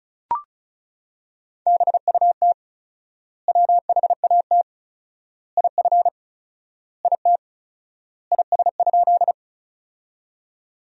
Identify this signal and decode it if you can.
Morse code, heard as a keyed tone, about 34 wpm, 700 Hz. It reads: BUT WHAT IF IT IS?
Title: 4x Word-Spacing - Sentences from Top 100 Words - 35wpm